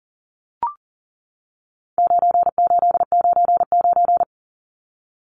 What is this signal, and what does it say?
Morse code, heard as a keyed tone, about 40 wpm, 700 Hz. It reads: 9899
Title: Numbers - 4-Digit 40wpm